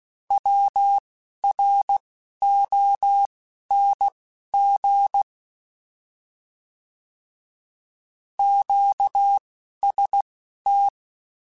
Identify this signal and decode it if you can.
Morse code, heard as a keyed tone, about 16 wpm, spaced as for 11 wpm, 800 Hz. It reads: WRONG QST
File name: Short Runs / Short Runs-mp3s / Words @10